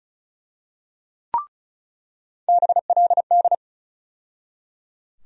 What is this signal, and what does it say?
Morse code, heard as a keyed tone, about 36 wpm, 700 Hz. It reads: BLD
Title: USA Call Sign Postfixes - 3 Letters 35wpm